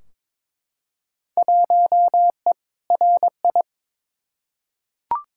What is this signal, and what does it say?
Morse code, heard as a keyed tone, about 22 wpm, 700 Hz. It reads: 1E RI
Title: Field Day Exchanges Encore - 22wpm